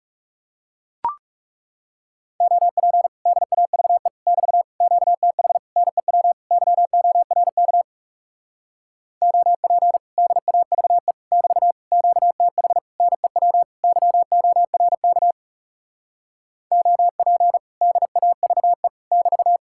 Morse code, heard as a keyed tone, about 37 wpm, 700 Hz. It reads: OP DAVE <BT> QTH NEW YORK OP DAVE <BT> QTH NEW YORK OP DAVE <BT>
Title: Speed-Racing Common QSO Phrases Encore - 30wpm